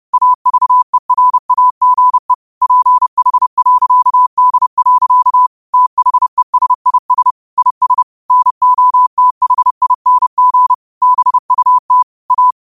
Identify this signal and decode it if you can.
Morse code, heard as a keyed tone, about 30 wpm, 1000 Hz. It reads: AVERAGE PH.D. THESIS IS NOTHING BUT A
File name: qotdfiles / 30wpm